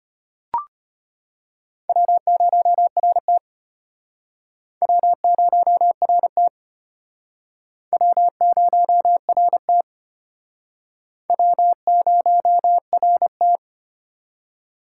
Morse code, heard as a keyed tone, about 33 wpm, 700 Hz. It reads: W0RT W0RT W0RT W0RT